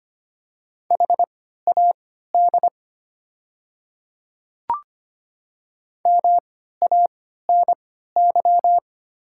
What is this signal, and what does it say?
Morse code, heard as a keyed tone, about 25 wpm, spaced as for 14 wpm, 700 Hz. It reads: HAD MANY
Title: 3x Character Spacing - Top 100 Words - 25wpm